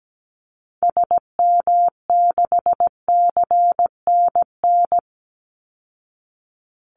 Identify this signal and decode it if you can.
Morse code, heard as a keyed tone, about 17 wpm, 700 Hz. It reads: SM6CNN
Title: Call Signs Repeated 3x - 17wpm